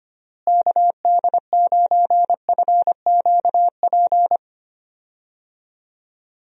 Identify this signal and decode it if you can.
Morse code, heard as a keyed tone, about 25 wpm, 700 Hz. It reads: KD9FQP
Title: USA Call Signs - 6 Characters - Repeated 2x - 25wpm